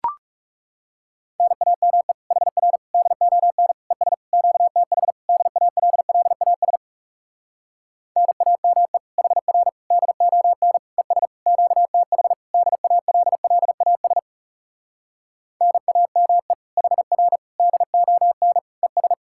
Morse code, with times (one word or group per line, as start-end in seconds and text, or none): 1.4-2.1 NAME
2.3-2.8 HR
2.9-3.7 DON
3.9-4.1 ES
4.3-5.1 QTH
5.3-6.8 DALLAS
8.2-9.0 NAME
9.2-9.7 HR
9.9-10.8 DON
11.0-11.3 ES
11.5-12.3 QTH
12.5-14.2 DALLAS
15.6-16.5 NAME
16.8-17.4 HR
17.6-18.6 DON
18.8-19.1 ES